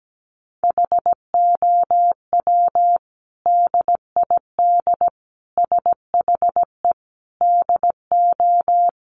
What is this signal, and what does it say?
Morse code, heard as a keyed tone, about 17 wpm, 700 Hz. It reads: HOW DID SHE DO
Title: Sentences from Top 100 Words 17wpm